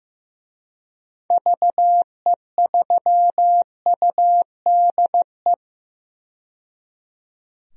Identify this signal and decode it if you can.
Morse code, heard as a keyed tone, about 15 wpm, 700 Hz. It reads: VE3UDE
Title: DXCC-100 Call Signs - 6 Characters - 15wpm